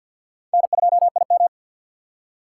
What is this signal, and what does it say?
Morse code, heard as a keyed tone, about 52 wpm, 700 Hz. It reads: N2IM
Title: USA Call Signs - 4 Characters - 2x Character Spacing - 50wpm